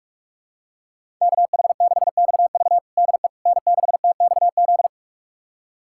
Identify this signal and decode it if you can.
Morse code, heard as a keyed tone, about 42 wpm, 700 Hz. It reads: KH6XV DE N6TXZ
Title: USA Callsign DE Callsign - 5 Characters - 45wpm